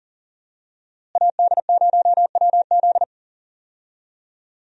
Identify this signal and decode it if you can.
Morse code, heard as a keyed tone, about 40 wpm, 700 Hz. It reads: AD0WZ
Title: USA Call Signs - 5 Characters - Repeated 2x - 40wpm